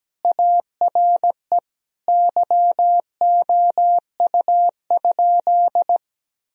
Morse code, heard as a keyed tone, about 17 wpm, 700 Hz. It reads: ARE YOU?
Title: Sentences from Top 1000 Words - Review Part 1 - 17wpm